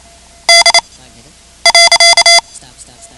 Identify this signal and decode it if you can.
Morse code, heard as a keyed tone, about 28 wpm, spaced as for 9 wpm, 2050 Hz. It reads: D.
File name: ham radio / CodeQuick / Code Quick 10